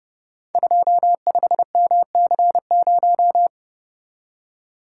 Morse code, heard as a keyed tone, about 30 wpm, 700 Hz. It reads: 25MC0
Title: Code Groups - 5 Letter:Numbers - 30wpm